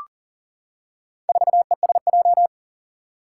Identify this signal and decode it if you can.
Morse code, heard as a keyed tone, about 40 wpm, 700 Hz. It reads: 4ESJ
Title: Four Challenging Letter-Number Combinations 40wpm